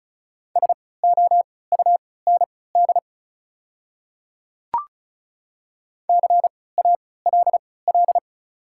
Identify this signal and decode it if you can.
Morse code, heard as a keyed tone, about 36 wpm, spaced as for 20 wpm, 690 Hz. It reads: SOUND CALL
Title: 3x Character Spacing - Top 300 Words - 35wpm